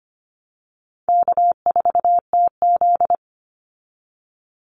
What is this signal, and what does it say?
Morse code, heard as a keyed tone, about 25 wpm, 700 Hz. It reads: K4TZ
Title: Easy Call Signs Repeated 3x - 25wpm